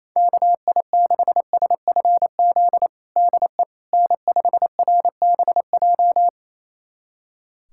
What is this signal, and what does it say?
Morse code, heard as a keyed tone, about 28 wpm, 700 Hz. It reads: KI6SFZ DE N5RBJ
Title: USA Callsign DE Callsign 28wpm